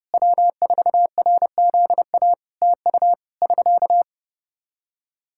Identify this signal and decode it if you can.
Morse code, heard as a keyed tone, about 30 wpm, 700 Hz. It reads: W4RZA TU <SK>
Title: Practice QSO - Exchange 4 - Ending - 30wpm